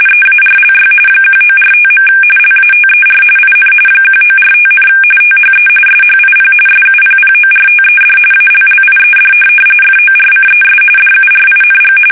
RTTY, about 45.45 baud. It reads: EARLY 1945. LATER IN 1945, THE SHIP SUPPORTED ALLIED FORCES DURING
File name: fsk850 / USS-Massachusetts